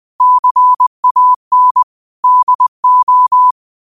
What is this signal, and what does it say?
Morse code, heard as a keyed tone, about 20 wpm, 1000 Hz. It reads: CAN DO